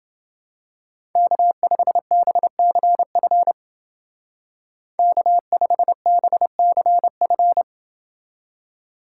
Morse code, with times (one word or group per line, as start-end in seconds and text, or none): 1.2-3.5 K5BCF
5.0-7.6 K5BCF